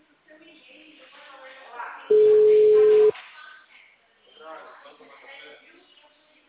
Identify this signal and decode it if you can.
Ringback tone (Europe, 425 Hz), with speech.